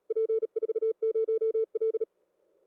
Morse code, heard as a keyed tone, about 35 wpm, 440 Hz. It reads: P40L